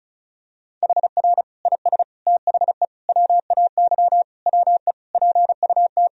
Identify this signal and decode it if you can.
Morse code, heard as a keyed tone, about 36 wpm, 700 Hz. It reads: HR IS THE WAY WE PUT
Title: Sentences from Top 200 Words and Abbreviations 35wpm